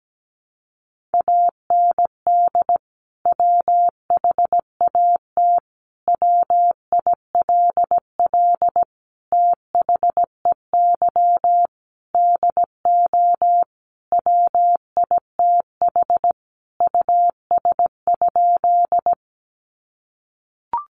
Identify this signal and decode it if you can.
Morse code, heard as a keyed tone, about 17 wpm, 700 Hz. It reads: AND WHAT WILL THEY DO WITH US?
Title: Sentences from Top 200 Words 17wpm